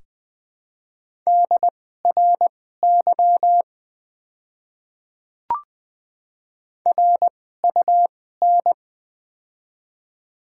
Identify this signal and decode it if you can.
Morse code, heard as a keyed tone, about 20 wpm, spaced as for 14 wpm, 700 Hz. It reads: DRY RUN
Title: 2x Character Spacing - Top 500 Words - 20wpm